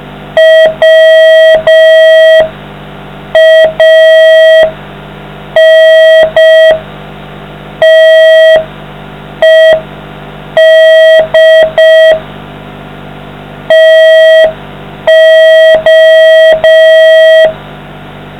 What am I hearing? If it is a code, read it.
Morse code, heard as a keyed tone, about 5 wpm, 640 Hz. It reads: WANTED TO